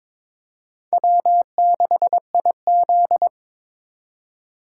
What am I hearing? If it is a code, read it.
Morse code, heard as a keyed tone, about 21 wpm, 700 Hz. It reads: W6IZ
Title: USA Call Signs - 4 Characters - 3x Character Spacing - 22wpm